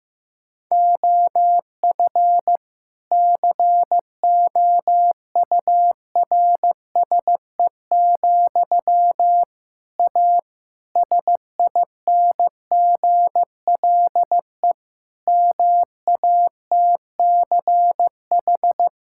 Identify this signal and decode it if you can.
Morse code, heard as a keyed tone, about 15 wpm, 700 Hz. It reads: OF COURSE, A SINGLE MATCH
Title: Sentences from Top 1600 Words 15wpm